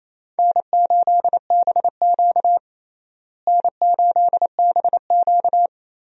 Morse code, heard as a keyed tone, about 28 wpm, 700 Hz. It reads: N8BQ N8BQ
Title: USA Call Signs - 4 Characters - Repeated 3x - 28wpm